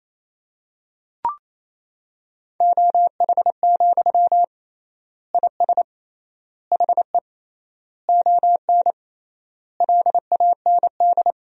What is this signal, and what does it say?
Morse code, heard as a keyed tone, about 28 wpm, 700 Hz. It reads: OH, IS HE ON LAND
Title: 3x Word-Spacing - Sentences from Top 500 Words - 28wpm